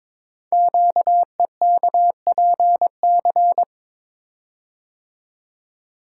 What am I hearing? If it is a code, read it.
Morse code, heard as a keyed tone, about 23 wpm, 700 Hz. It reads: QEKPC
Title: Code Groups - 5 Letters - 22wpm